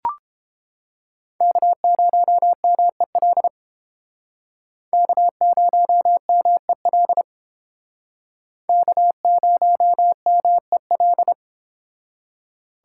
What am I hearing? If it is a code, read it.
Morse code, heard as a keyed tone, about 31 wpm, 700 Hz. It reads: K0MEL K0MEL K0MEL